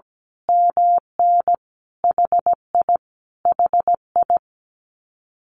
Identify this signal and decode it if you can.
Morse code, heard as a keyed tone, about 17 wpm, 700 Hz. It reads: MN HI HI
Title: Common QSO Phrases Encore 17wpm